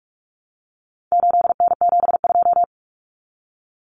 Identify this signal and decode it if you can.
Morse code, heard as a keyed tone, about 44 wpm, 700 Hz. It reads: 8N72
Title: Common Cut Numbers - 4-Digit 45wpm